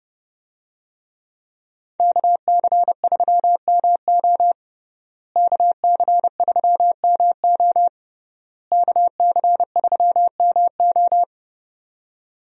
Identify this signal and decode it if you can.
Morse code, heard as a keyed tone, about 30 wpm, 700 Hz. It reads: KC3MO KC3MO KC3MO
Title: USA Call Signs - 5 Characters - Repeated 3x - 30wpm